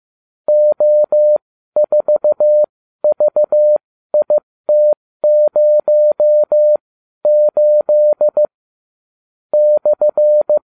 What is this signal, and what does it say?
Morse code, heard as a keyed tone, about 15 wpm, spaced as for 12 wpm, 600 Hz. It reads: O4VIT08 /